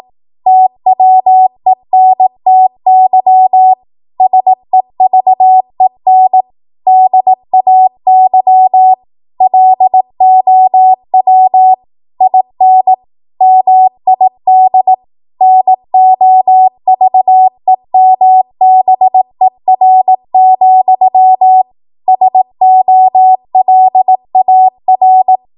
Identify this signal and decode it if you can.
Morse code, heard as a keyed tone, about 18 wpm, 750 Hz. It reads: TWENTY SEVEN DAY LOW IN MID NOVEMBER, SOLAR
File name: Archive / 18 WPM / 18WPM